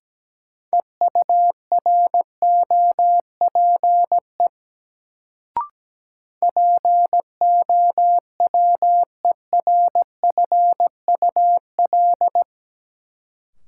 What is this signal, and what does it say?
Morse code, heard as a keyed tone, about 17 wpm, 700 Hz. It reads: EUROPE POWERFUL
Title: Sets of 5 Words Encore - New Word Drill - 17wpm